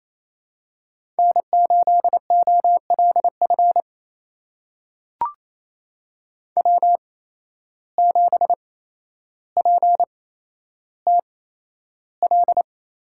Morse code, heard as a keyed tone, about 28 wpm, spaced as for 12 wpm, 700 Hz. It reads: N8OLF W7PTL
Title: USA Call Signs - 5 Characters - 8x Character Spacing - 28wpm